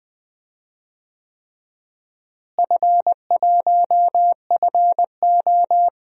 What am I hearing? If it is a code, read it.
Morse code, heard as a keyed tone, about 21 wpm, 700 Hz. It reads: F1FO
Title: DXCC-100 Call Signs - 4 Characters - Repeated 3x - 20wpm